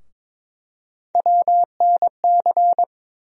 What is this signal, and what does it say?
Morse code, heard as a keyed tone, about 22 wpm, 700 Hz. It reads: WNC
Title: USA Call Sign Postfixes - 3 Letters 22wpm